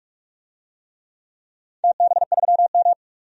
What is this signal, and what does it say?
Morse code, heard as a keyed tone, about 46 wpm, 700 Hz. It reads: TB3M
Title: DXCC-100 Call Signs - 4 Characters - 45wpm